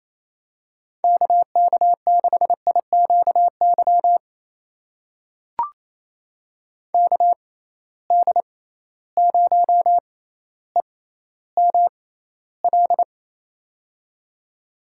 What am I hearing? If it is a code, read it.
Morse code, heard as a keyed tone, about 28 wpm, spaced as for 14 wpm, 700 Hz. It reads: KK6IQY KD0EML
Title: USA Call Signs - 6 Characters - 6x Character Spacing - 28wpm